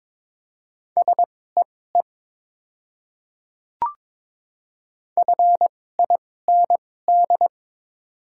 Morse code, heard as a keyed tone, about 22 wpm, spaced as for 16 wpm, 690 Hz. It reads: SEE FIND
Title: 2x Character Spacing - Top 100 Words - 22wpm